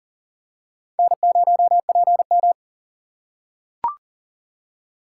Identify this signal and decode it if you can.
Morse code, heard as a keyed tone, about 38 wpm, 700 Hz. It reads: N0PM
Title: USA Call Signs - 4 Characters - 6x Character Spacing - 40wpm